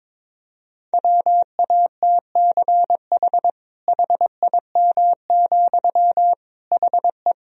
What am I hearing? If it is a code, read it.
Morse code, heard as a keyed tone, about 22 wpm, 700 Hz. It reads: WATCH HIM, HE
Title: Sentences from Top 300 Words and Abbreviations 22wpm